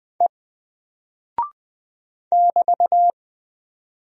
Morse code, heard as a keyed tone, about 20 wpm, 700 Hz. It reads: E <BT>
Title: Single Character (Letter:Number:Punctuation:Prosign) 20wpm